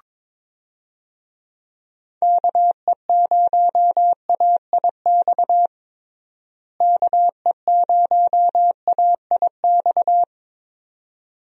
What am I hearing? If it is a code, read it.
Morse code, heard as a keyed tone, about 22 wpm, 700 Hz. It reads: KE0AIX KE0AIX
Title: USA Call Signs - 6 Characters - Repeated 2x - 22wpm